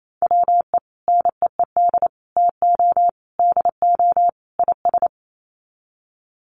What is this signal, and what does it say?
Morse code, heard as a keyed tone, about 28 wpm, 700 Hz. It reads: WE NEED TO DO IS
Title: Sets of 5 Words Encore2 28wpm